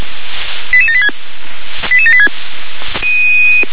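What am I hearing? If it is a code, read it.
Five-tone selective call (ZVEI), heard as quick runs of five tones, tones of about 70 ms. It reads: 88786 88786
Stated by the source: ZVEI five-tone selective call received on an SDR.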